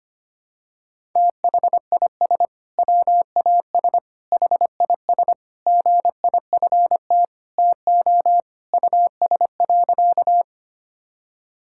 Morse code, heard as a keyed tone, about 25 wpm, 700 Hz. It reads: THIS WAS HIS GIFT TO US.
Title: Sentences from Top 1000 Words - Review Part 6 - 25wpm